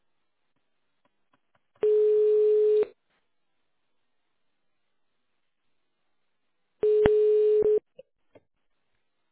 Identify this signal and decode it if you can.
Ringback tone (Europe, 425 Hz).